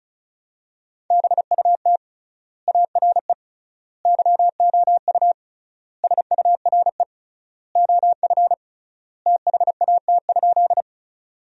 Morse code, heard as a keyed tone, about 36 wpm, 700 Hz. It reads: BUT ARE YOU SURE OF THAT?
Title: 3x Word-Spacing - Sentences from Top 300 Words - 35wpm